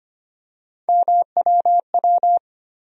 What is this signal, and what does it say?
Morse code, heard as a keyed tone, about 25 wpm, 700 Hz. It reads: MWW
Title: USA Call Sign Postfixes - 3 Letters 25wpm